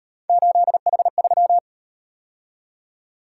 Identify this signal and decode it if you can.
Morse code, heard as a keyed tone, about 38 wpm, 700 Hz. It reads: 8H3